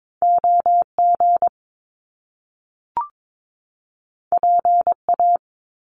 Morse code, heard as a keyed tone, about 22 wpm, 700 Hz. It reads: OG PA
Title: Top Two Letter Combinations in Words 22wpm